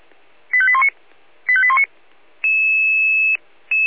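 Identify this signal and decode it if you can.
Five-tone selective call (ZVEI), heard as quick runs of five tones, tones of about 70 ms. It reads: 85819 85819
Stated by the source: ZVEI five-tone call received by SDR.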